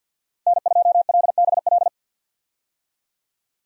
Morse code, heard as a keyed tone, about 50 wpm, 700 Hz. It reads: N2LBL
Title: USA Call Signs - 5 Characters - 50wpm